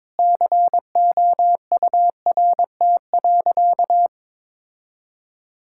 Morse code, heard as a keyed tone, about 22 wpm, 700 Hz. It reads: COURT.
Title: Sentences from Top 1600 Words 22wpm